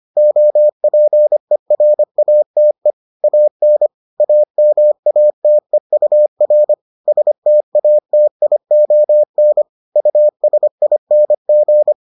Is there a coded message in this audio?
Yes: Morse code from 0.2 to 11.9 s, heard as a keyed tone, about 25 wpm, 600 Hz. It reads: OPERATE AN AMATEUR STATION USING